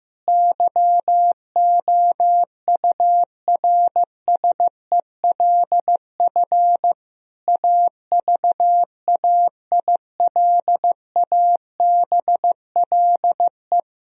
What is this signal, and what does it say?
Morse code, heard as a keyed tone, about 15 wpm, 700 Hz. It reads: YOURSELF AVAILABLE
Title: Sentences from Top 1000 Words - Review Part 3 - 15wpm